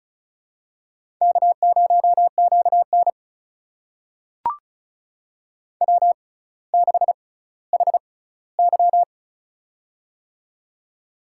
Morse code, heard as a keyed tone, about 36 wpm, spaced as for 18 wpm, 700 Hz. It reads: K0QN W6HY
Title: USA Call Signs - 4 Characters - 6x Character Spacing - 35wpm